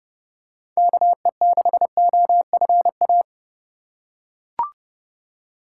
Morse code, heard as a keyed tone, about 32 wpm, 700 Hz. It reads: KE6OFA
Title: USA Call Signs - 6 Characters - 4x Character Spacing - 30wpm